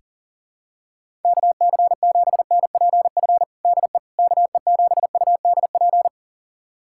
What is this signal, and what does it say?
Morse code, heard as a keyed tone, about 40 wpm, 700 Hz. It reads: KC7NPF DE KE7UDP
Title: USA Callsign DE Callsign - 6 Characters - 40wpm